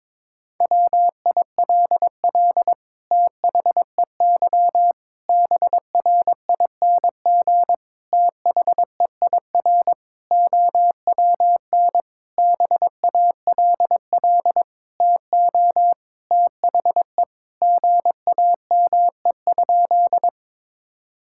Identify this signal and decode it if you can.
Morse code, heard as a keyed tone, about 21 wpm, 700 Hz. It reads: WILL THEY BRING THEIR OWN BALL TO THE GAME?
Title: Sentences from Top 1000 Words - Review Part 6 - 22wpm